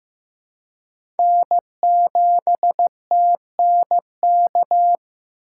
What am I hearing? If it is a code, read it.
Morse code, heard as a keyed tone, about 16 wpm, 700 Hz. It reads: N7TNK
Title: USA Call Signs - 5 Characters - 4x Character Spacing - 15wpm